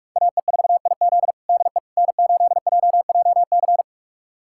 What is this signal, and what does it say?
Morse code, heard as a keyed tone, about 46 wpm, 700 Hz. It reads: AE4IZ DE N8JJC
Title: USA Callsign DE Callsign - 5 Characters - 45wpm